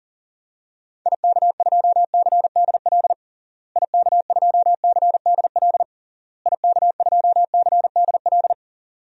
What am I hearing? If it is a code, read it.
Morse code, heard as a keyed tone, about 40 wpm, 700 Hz. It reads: IK2CDL IK2CDL IK2CDL
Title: DXCC-100 Call Signs - 6 Characters - Repeated 3x - 40wpm